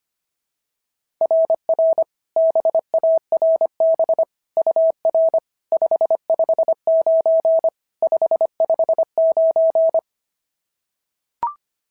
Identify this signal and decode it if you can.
Morse code, heard as a keyed tone, about 25 wpm, 690 Hz. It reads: RR BARB UR 559 559
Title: SOTA QSO Practice 25wpm